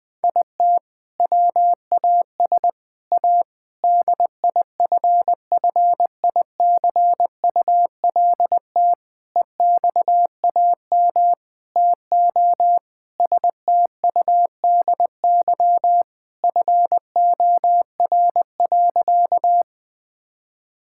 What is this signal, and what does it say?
Morse code, heard as a keyed tone, about 20 wpm, 700 Hz. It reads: IT WAS A DIFFICULT EXAM TO STUDY FOR.
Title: Sentences from Top 1000 Words - Review Part 5 - 20wpm